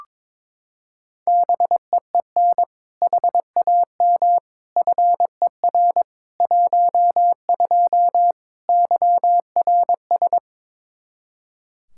Morse code, heard as a keyed tone, about 22 wpm, 700 Hz. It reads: BEEN HAM FER 12 YRS
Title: Practice QSO - Exchange 3 - More Info - 22wpm